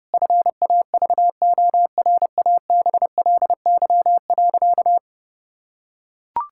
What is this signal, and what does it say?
Morse code, heard as a keyed tone, about 30 wpm, 700 Hz. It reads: FAVORABLY.
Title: Pangram Challenge v2 - 30wpm